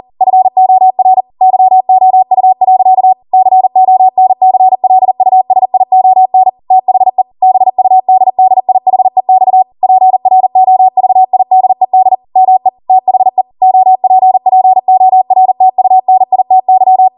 Morse code, heard as a keyed tone, about 40 wpm, 750 Hz. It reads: FOR YOU. CONCLUSION THE BUDDIHEX PROVIDED ME THE OPPORTUNITY
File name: Morse / 40WPM